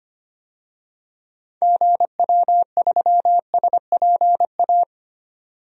Morse code, heard as a keyed tone, about 25 wpm, 700 Hz. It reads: GW3SPA
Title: DXCC-100 Call Signs - 6 Characters - 25wpm